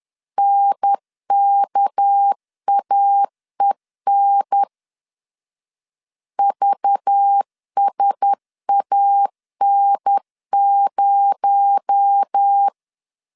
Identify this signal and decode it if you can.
Morse code, heard as a keyed tone, about 10 wpm, 800 Hz. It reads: NKAEN VSAN0
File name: CW / DET / 1 6 1500C S B 10wpm 4137